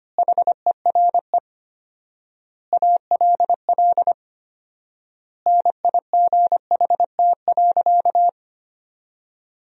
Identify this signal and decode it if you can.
Morse code, heard as a keyed tone, about 25 wpm, 700 Hz. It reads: HERE ALL NIGHT.
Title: 4x Word-Spacing - Sentences from Top 300 Words - 25wpm